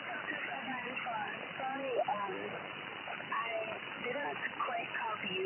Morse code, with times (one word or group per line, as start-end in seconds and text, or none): none